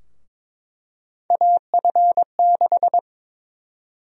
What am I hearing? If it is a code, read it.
Morse code, heard as a keyed tone, about 22 wpm, 700 Hz. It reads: AF6
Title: USA Call Sign Prefixes - Repeated 2x - 22wpm